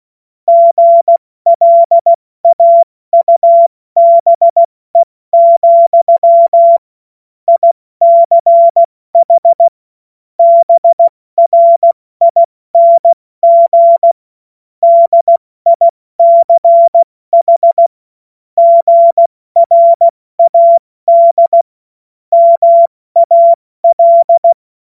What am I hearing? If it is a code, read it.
Morse code, heard as a keyed tone, about 16 wpm, spaced as for 14 wpm, 680 Hz. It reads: GLAUBE, ICH BRING DICH GRAD MAL
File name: Morsekurs / Witze / Witz 24. August 2021, 70 F80